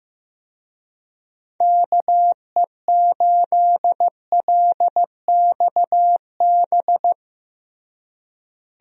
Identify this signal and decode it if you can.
Morse code, heard as a keyed tone, about 16 wpm, 700 Hz. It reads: KE8LXB
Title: USA Call Signs - 6 Characters - Repeated 2x - 15wpm